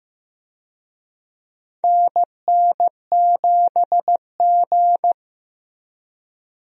Morse code, heard as a keyed tone, about 16 wpm, 700 Hz. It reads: NN7G
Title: USA Call Signs - 4 Characters - Repeated 2x - 15wpm